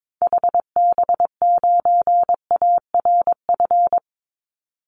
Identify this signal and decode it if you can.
Morse code, heard as a keyed tone, about 22 wpm, 700 Hz. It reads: HB9ARF
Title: Call Signs Repeated 3x - 22wpm